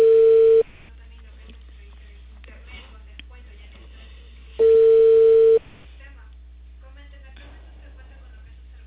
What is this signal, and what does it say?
Ringback tone (China, 450 Hz), with speech.